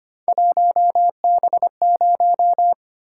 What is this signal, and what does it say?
Morse code, heard as a keyed tone, about 25 wpm, 700 Hz. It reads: 1B0
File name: Three Letter-Number Combinations - 25wpm